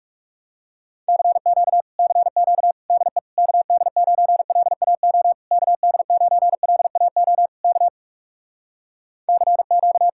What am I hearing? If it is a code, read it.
Morse code, heard as a keyed tone, about 46 wpm, 700 Hz. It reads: CQ CQ DE KD9LAO KD9LAO K CQ